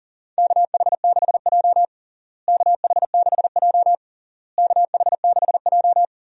Morse code, heard as a keyed tone, about 40 wpm, 700 Hz. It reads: KH6J KH6J KH6J
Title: USA Call Signs - 4 Characters - Repeated 3x - 40wpm